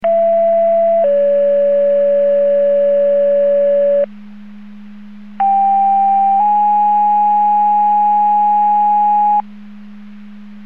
Two-tone page, heard as a one-second tone, then a three-second tone, about 670 then 570 Hz, about 790 then 830 Hz. It is heard from 0.0 to 9.4 s.
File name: Portage 154.385 Paging